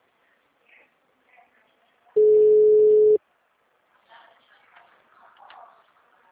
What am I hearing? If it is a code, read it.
Ringback tone (Europe, 425 Hz).